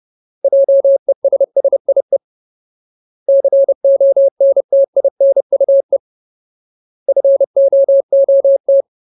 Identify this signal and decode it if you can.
Morse code, heard as a keyed tone, about 30 wpm, 550 Hz. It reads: JESSIE CONTINUE FOOT
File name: Ultimate Word List / 30WPM Ultimate